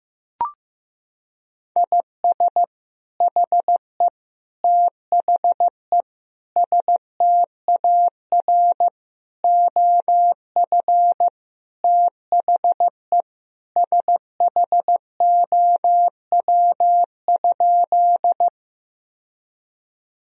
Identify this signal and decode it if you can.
Morse code, heard as a keyed tone, about 15 wpm, 700 Hz. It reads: IS HE THE STAR OF THE SHOW?
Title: Sentences from Top 1000 Words - Review Part 2 - 15wpm